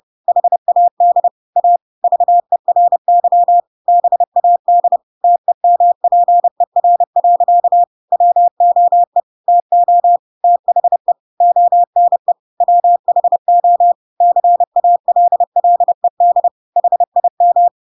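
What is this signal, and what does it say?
Morse code, heard as a keyed tone, about 30 wpm, 700 Hz. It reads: HAD A VERY BAD TEMPER. WOE TO THE ONE WHO CALLED HIM
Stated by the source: Chapter 2 of The Adventures of Pinocchio read as a book in Morse code (CW) at 30 wpm, by ik1hge.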